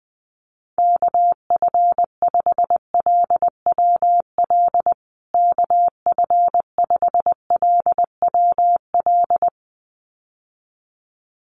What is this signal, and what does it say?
Morse code, heard as a keyed tone, about 20 wpm, 700 Hz. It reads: KF5LWL KF5LWL
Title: USA Call Signs Repeated 2x 20wpm